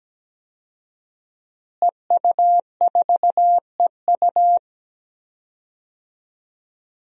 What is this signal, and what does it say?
Morse code, heard as a keyed tone, about 17 wpm, 700 Hz. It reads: EU4EU
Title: DXCC-100 Call Signs - 5 Characters - Repeated 2x - 17wpm